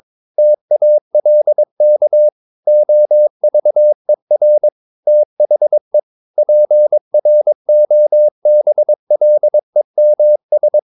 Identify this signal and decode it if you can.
Morse code, heard as a keyed tone, about 22 wpm, 600 Hz. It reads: TALK OVER THE PROBLEMS
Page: Romanen Never af Ken Follet på engelsk.